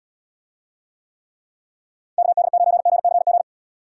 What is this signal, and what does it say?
Morse code, heard as a keyed tone, about 72 wpm, 700 Hz. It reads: KD9GCG